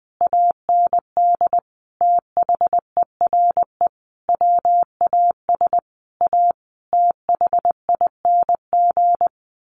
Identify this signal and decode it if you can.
Morse code, heard as a keyed tone, about 20 wpm, 700 Hz. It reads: AND THERE WAS A THING
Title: A Sound Of Thunder 20wpm